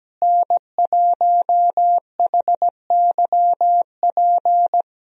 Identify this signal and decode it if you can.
Morse code, heard as a keyed tone, about 17 wpm, 700 Hz. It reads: N1HYP
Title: USA Call Signs - 5 Characters - Repeated 2x - 17wpm